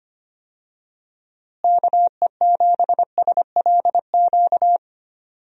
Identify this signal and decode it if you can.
Morse code, heard as a keyed tone, about 25 wpm, 700 Hz. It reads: KE7SLQ
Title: USA Call Signs - 6 Characters - Repeated 3x - 25wpm